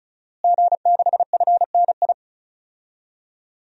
Morse code, heard as a keyed tone, about 34 wpm, 700 Hz. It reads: G6FNI
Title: DXCC-100 Call Signs - 5 Characters - Repeated 3x - 35wpm